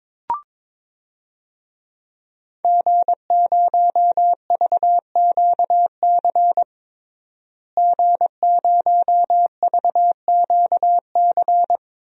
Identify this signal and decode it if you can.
Morse code, heard as a keyed tone, about 22 wpm, 700 Hz. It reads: G0VQC G0VQC
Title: DXCC-100 Call Signs - 5 Characters - Repeated 2x - 22wpm